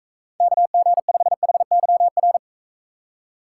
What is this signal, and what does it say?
Morse code, heard as a keyed tone, about 41 wpm, 700 Hz. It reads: KG5HYR